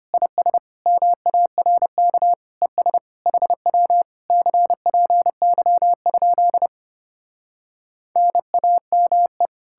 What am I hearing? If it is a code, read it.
Morse code, heard as a keyed tone, about 33 wpm, 700 Hz. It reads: IS MARK ES HW CPY? NAME